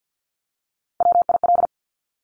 Morse code, heard as a keyed tone, about 50 wpm, 700 Hz. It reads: WIL
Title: Top Three Letter Combinations in Words 50wpm